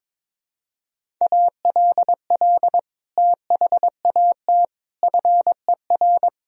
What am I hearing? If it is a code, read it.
Morse code, heard as a keyed tone, about 22 wpm, 700 Hz. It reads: ALL THAT FER
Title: Sentences from Top 500 Words and Abbreviations 22wpm